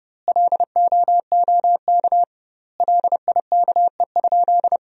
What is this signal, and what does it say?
Morse code, heard as a keyed tone, about 30 wpm, 700 Hz. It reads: LOOK LIKE?
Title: 2x Word-Spacing - Sentences from Top 100 Words - 30wpm